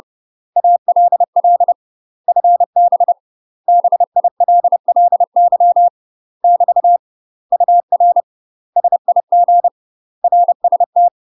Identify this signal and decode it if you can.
Morse code, heard as a keyed tone, about 30 wpm, 700 Hz. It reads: ALL FB BILLY <BT> UR SIG RST
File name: UK MASTER QSOs TRAINING/30WPM